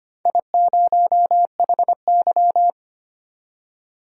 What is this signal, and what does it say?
Morse code, heard as a keyed tone, about 25 wpm, 700 Hz. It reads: I0HY
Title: DXCC-100 Call Signs - 4 Characters - 25wpm